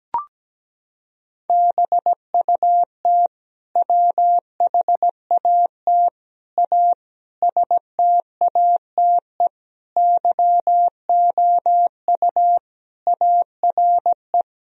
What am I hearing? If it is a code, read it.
Morse code, heard as a keyed tone, about 17 wpm, 700 Hz. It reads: BUT WHAT A STATE YOU ARE
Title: Sentences from Top 1000 Words - Review Part 2 - 17wpm